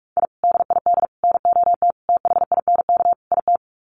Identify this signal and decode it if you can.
Morse code, heard as a keyed tone, about 44 wpm, 700 Hz. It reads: I DID NOT THINK IT
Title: Sets of 5 Words Encore2 45wpm